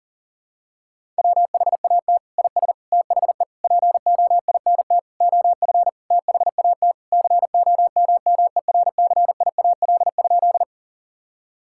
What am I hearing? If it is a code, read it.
Morse code, heard as a keyed tone, about 40 wpm, 700 Hz. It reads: WHAT IS THE POINT OF THAT COMMERCIAL?
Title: Sentences from Top 1000 Words - Review Part 5 - 40wpm